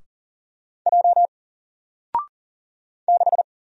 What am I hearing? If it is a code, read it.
Morse code, heard as a keyed tone, about 40 wpm, 700 Hz. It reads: J 6
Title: Single Letter-Number 40wpm